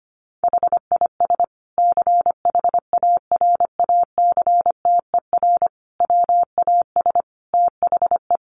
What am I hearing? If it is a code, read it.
Morse code, heard as a keyed tone, about 25 wpm, 700 Hz. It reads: HIS CHARACTER WAS THE
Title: Sentences from Top 1300 Words 25wpm